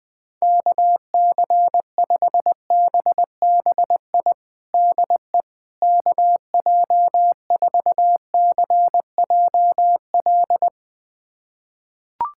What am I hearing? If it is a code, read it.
Morse code, heard as a keyed tone, about 20 wpm, 700 Hz. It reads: KC5BBI DE KJ4CJL
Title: USA Callsign DE Callsign - 6 Characters - 20wpm